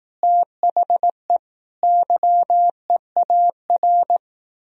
Morse code, heard as a keyed tone, about 18 wpm, 700 Hz. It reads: THE YEAR
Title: Speed-Racing - Sets of 5 Words 15wpm